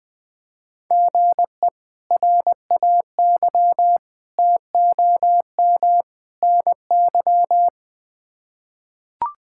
Morse code, heard as a keyed tone, about 20 wpm, 700 Hz. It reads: GE RAY TOM NY
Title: K1USN SST QSO Practice - 20wpm